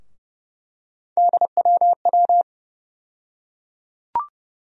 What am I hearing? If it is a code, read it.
Morse code, heard as a keyed tone, about 32 wpm, 700 Hz. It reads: DWW
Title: USA Call Sign Postfixes - 3 Letters 30wpm